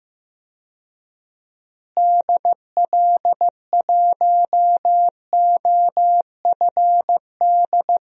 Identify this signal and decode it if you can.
Morse code, heard as a keyed tone, about 15 wpm, 700 Hz. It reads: DL1OFD
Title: DXCC-100 Call Signs - 6 Characters - Repeated 3x - 15wpm